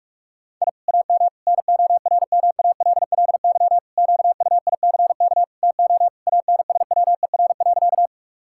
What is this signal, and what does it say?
Morse code, heard as a keyed tone, about 46 wpm, 700 Hz. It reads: I AM NORMALLY QUICK TO ANSWER.
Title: Sentences from Top 1000 Words - Review Part 6 - 45wpm